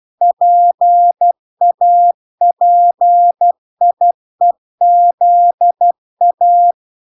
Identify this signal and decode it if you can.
Morse code, heard as a keyed tone, about 12 wpm, 700 Hz. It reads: PAPIEZA